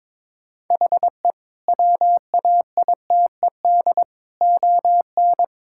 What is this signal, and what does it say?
Morse code, heard as a keyed tone, about 22 wpm, 700 Hz. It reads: HE WAITED ON
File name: practice / 22wpm / Sentences with 4-Word Phrases - 22wpm